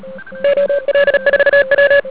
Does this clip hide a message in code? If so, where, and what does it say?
Yes: Morse code from 0.4 to 2.0 s, heard as a keyed tone, about 37 wpm, 560 Hz. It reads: OL4W